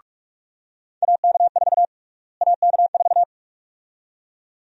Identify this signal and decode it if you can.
Morse code, heard as a keyed tone, about 41 wpm, 700 Hz. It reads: AK4 AK4
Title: USA Call Sign Prefixes - Repeated 2x - 45wpm